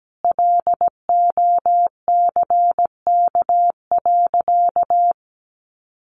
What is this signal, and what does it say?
Morse code, heard as a keyed tone, about 17 wpm, 700 Hz. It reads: LOCK.
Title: Sentences from Top 1300 Words 17wpm